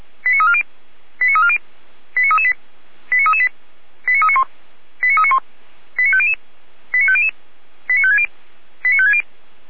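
Five-tone selective call (ZVEI), heard as quick runs of five tones, tones of about 80 ms. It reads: 89340 89340 89338 89338 89391 89391 89500 89500 89560 89560
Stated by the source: Received on an SDR.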